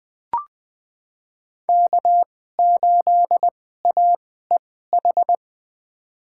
Morse code, heard as a keyed tone, about 20 wpm, spaced as for 15 wpm, 700 Hz. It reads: K8AEH
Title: USA Call Signs - 5 Characters - 2x Character Spacing - 20wpm